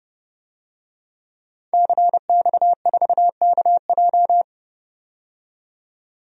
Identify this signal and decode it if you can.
Morse code, heard as a keyed tone, about 31 wpm, 700 Hz. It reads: CX4KJ
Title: DXCC-100 Call Signs - 5 Characters - 30wpm